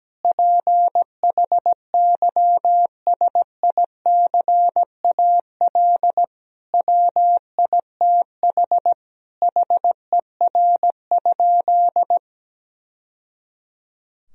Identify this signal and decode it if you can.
Morse code, heard as a keyed tone, about 17 wpm, 700 Hz. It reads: PHYSICAL WITH HER?
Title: Sentences from Top 1000 Words - Review Part 4 - 17wpm